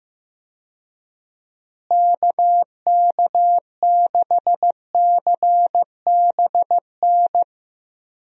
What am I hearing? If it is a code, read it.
Morse code, heard as a keyed tone, about 15 wpm, 700 Hz. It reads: KK6CBN
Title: USA Call Signs - 6 Characters - Repeated 2x - 15wpm